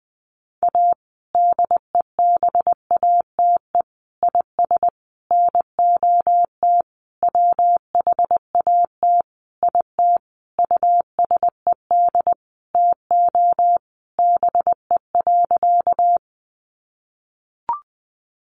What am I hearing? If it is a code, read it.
Morse code, heard as a keyed tone, about 20 wpm, 700 Hz. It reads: A DEBATE IS NOT WHAT IT USED TO BE.
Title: Sentences from Top 1100 Words 20wpm